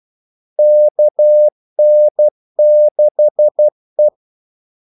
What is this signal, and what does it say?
Morse code, heard as a keyed tone, about 12 wpm, 600 Hz. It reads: KN6E
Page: 2 ekstra ordmellemrum mellem kaldesignalerne